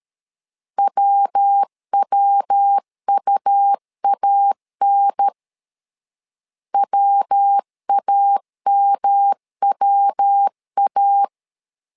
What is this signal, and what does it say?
Morse code, heard as a keyed tone, about 13 wpm, 800 Hz. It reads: WWUAN WAMWA
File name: LEZIONI / 3 MWRO / 3 MW 12wpm 1633